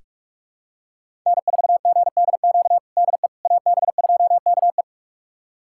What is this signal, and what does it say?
Morse code, heard as a keyed tone, about 41 wpm, 700 Hz. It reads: N4GDQ DE AB2KE
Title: USA Callsign DE Callsign - 5 Characters - 45wpm